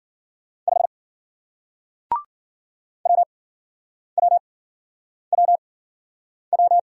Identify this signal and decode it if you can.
Morse code, heard as a keyed tone, about 55 wpm, spaced as for 9 wpm, 700 Hz. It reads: 5 WWWW